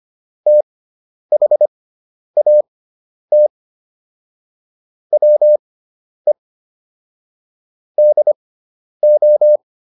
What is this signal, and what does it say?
Morse code, heard as a keyed tone, about 25 wpm, spaced as for 10 wpm, 600 Hz. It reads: THAT WE DO
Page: intermediate-practice-files